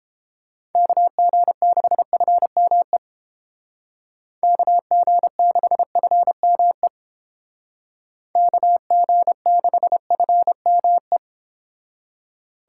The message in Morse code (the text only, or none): KG6FME KG6FME KG6FME